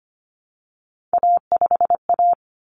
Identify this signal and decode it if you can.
Morse code, heard as a keyed tone, about 25 wpm, 700 Hz. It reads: A5A
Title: Common Cut Numbers - 3-Digit 25wpm